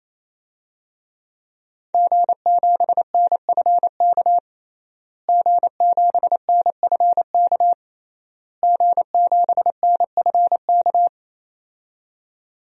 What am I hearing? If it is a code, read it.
Morse code, heard as a keyed tone, about 28 wpm, 700 Hz. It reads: G7NFK G7NFK G7NFK
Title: DXCC-100 Call Signs - 5 Characters - Repeated 3x - 28wpm